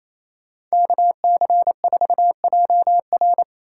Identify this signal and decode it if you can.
Morse code, heard as a keyed tone, about 37 wpm, 700 Hz. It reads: KC4JR